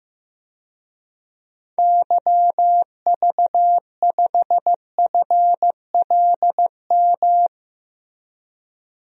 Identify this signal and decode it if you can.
Morse code, heard as a keyed tone, about 15 wpm, 700 Hz. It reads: YV5FLM
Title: DXCC-100 Call Signs - 6 Characters - Repeated 3x - 15wpm